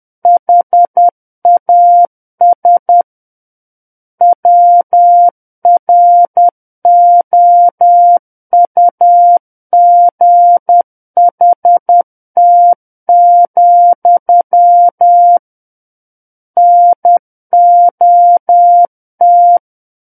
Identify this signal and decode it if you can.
Morse code, heard as a keyed tone, about 10 wpm, 700 Hz. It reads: HAS WROUGHT, NOT
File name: Book One / 01-Chapter 1 - 10 WPM